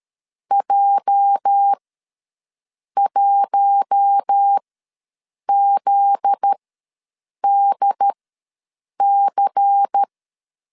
Morse code, heard as a keyed tone, about 13 wpm, spaced as for 7 wpm, 800 Hz. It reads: J1ZDC
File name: LEZIONI / 8 YJ45 / 8 YJ ws 12wpm 3342